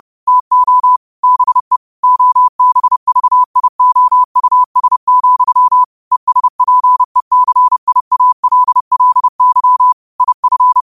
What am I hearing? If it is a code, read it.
Morse code, heard as a keyed tone, about 30 wpm, 1000 Hz. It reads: TO BE OBVIOUS, ESPECIALLY IF
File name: qotdfiles / 30wpm